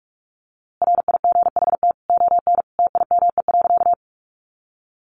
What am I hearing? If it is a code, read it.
Morse code, heard as a keyed tone, about 44 wpm, 700 Hz. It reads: RIGHT ON TIME.
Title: A Sound Of Thunder 45wpm